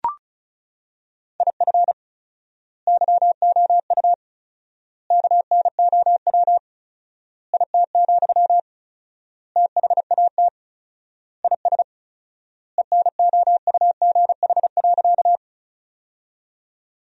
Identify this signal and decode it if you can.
Morse code, heard as a keyed tone, about 34 wpm, 700 Hz. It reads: IF YOU KNOW IT, THAT IS ENOUGH.
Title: 4x Word-Spacing - Sentences from Top 400 Words - 35wpm